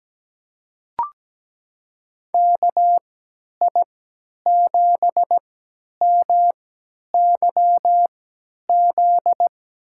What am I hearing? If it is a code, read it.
Morse code, heard as a keyed tone, about 17 wpm, spaced as for 10 wpm, 700 Hz. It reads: KI7MYZ
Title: USA Call Signs - 3x Character Spacing - 17wpm